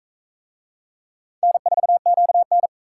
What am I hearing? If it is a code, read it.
Morse code, heard as a keyed tone, about 38 wpm, 700 Hz. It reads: N4QN